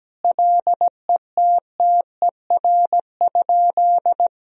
Morse code, heard as a keyed tone, about 17 wpm, 700 Hz. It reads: LETTER?
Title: Sentences from Top 300 Words and Abbreviations 17wpm